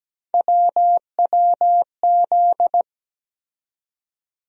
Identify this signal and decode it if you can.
Morse code, heard as a keyed tone, about 17 wpm, 700 Hz. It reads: WWZ
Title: USA Call Sign Postfixes - 3 Letters 17wpm